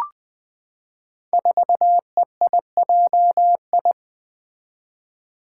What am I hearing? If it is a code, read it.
Morse code, heard as a keyed tone, about 20 wpm, 700 Hz. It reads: 4EIJI
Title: Five Challenging Letter-Number Combinations 20wpm